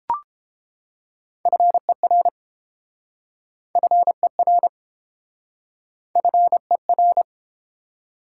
Morse code, heard as a keyed tone, about 30 wpm, 700 Hz. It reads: FER FER FER